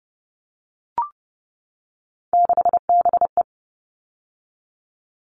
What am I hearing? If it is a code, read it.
Morse code, heard as a keyed tone, about 30 wpm, 700 Hz. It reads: 6BE
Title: All Cut Numbers - 3-Digit 30wpm